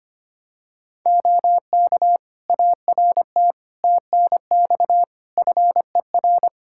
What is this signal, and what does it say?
Morse code, heard as a keyed tone, about 25 wpm, 700 Hz. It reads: OK ART TNX FER
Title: Practice QSO - Exchange 4 - Ending - 25wpm